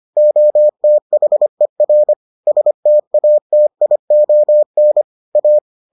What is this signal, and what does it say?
Morse code, heard as a keyed tone, about 25 wpm, 600 Hz. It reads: OTHER STATION A